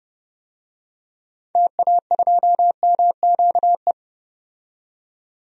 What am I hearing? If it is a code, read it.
Morse code, heard as a keyed tone, about 30 wpm, 700 Hz. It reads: TA2MQE
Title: DXCC-100 Call Signs - 6 Characters - 30wpm